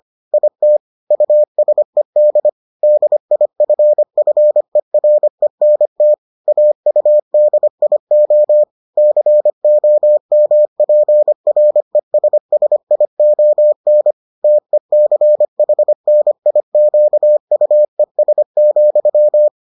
Morse code, heard as a keyed tone, about 25 wpm, 600 Hz. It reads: IT USED DIFFERENT AUDIO COMPRESSION TECHNIQUES,